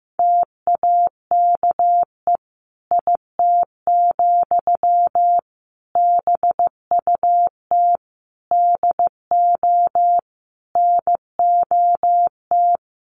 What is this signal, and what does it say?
Morse code, heard as a keyed tone, about 15 wpm, 700 Hz. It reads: TAKE IT, BUT DO NOT
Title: Sentences from Top 1500 Words 15wpm